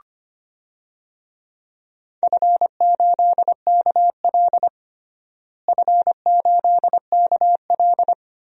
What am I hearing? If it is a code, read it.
Morse code, heard as a keyed tone, about 25 wpm, 700 Hz. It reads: F8KL F8KL
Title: DXCC-100 Call Signs - 4 Characters - Repeated 3x - 25wpm